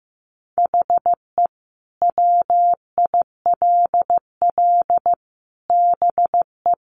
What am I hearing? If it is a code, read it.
Morse code, heard as a keyed tone, about 15 wpm, 700 Hz. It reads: HE WILL BE
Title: Sentences from Top 800 Words 15wpm